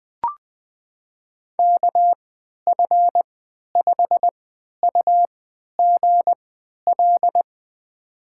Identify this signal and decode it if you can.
Morse code, heard as a keyed tone, about 20 wpm, spaced as for 11 wpm, 700 Hz. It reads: KF5UGL
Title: USA Call Signs - 6 Characters - 3x Character Spacing - 20wpm